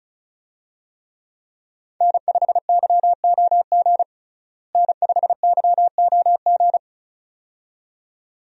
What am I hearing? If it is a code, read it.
Morse code, heard as a keyed tone, about 34 wpm, 700 Hz. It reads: N5YOG N5YOG
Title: USA Call Signs - 5 Characters - Repeated 2x - 35wpm